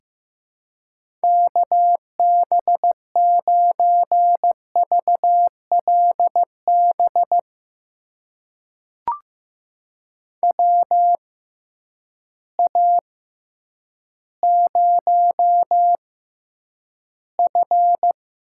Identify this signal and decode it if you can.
Morse code, heard as a keyed tone, about 15 wpm, spaced as for 9 wpm, 700 Hz. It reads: KB9VLB WA0F
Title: USA Call Signs - 6x Character Spacing - 15wpm